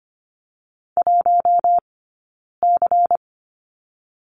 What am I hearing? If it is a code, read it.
Morse code, heard as a keyed tone, about 25 wpm, 700 Hz. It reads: 1 C
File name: course / 25:10wpm / 092 Lesson 16d - Letter c - Two Characters 25wpm